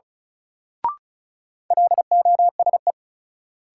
Morse code, heard as a keyed tone, about 36 wpm, 700 Hz. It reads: LOSE
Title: Sentences from Top 1100 Words - New Word Drill - 35wpm